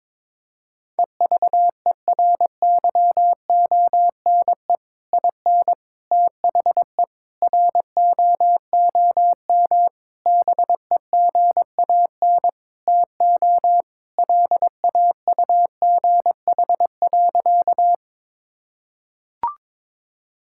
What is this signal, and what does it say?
Morse code, heard as a keyed tone, about 22 wpm, 700 Hz. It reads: EVERYONE IN THE ROOM BEGAN TO LAUGH.
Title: Sentences from Top 1000 Words - Review Part 5 - 22wpm